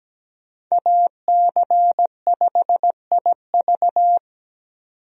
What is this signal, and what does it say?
Morse code, heard as a keyed tone, about 17 wpm, 700 Hz. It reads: AC5IV
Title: USA Call Signs - 5 Characters - 17wpm